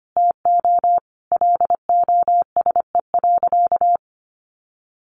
Morse code, heard as a keyed tone, about 25 wpm, 700 Hz. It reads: TO LOSE.
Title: Sentences from Top 1100 Words 25wpm